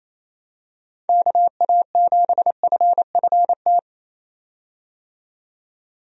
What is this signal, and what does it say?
Morse code, heard as a keyed tone, about 28 wpm, 700 Hz. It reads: KA7FFT
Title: USA Call Signs - 6 Characters - 28wpm